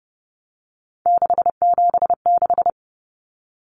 Morse code, heard as a keyed tone, about 30 wpm, 700 Hz. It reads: 676
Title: Numbers - 3-Digit 30wpm